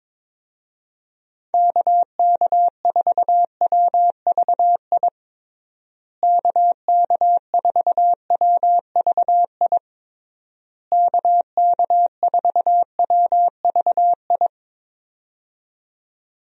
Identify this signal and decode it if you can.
Morse code, heard as a keyed tone, about 22 wpm, 700 Hz. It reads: KK4WVI KK4WVI KK4WVI
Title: USA Call Signs - 6 Characters - Repeated 3x - 22wpm